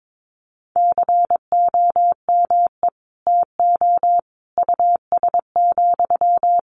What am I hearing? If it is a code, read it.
Morse code, heard as a keyed tone, about 22 wpm, 700 Hz. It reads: COME TO US,
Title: Sentences from Top 1300 Words 22wpm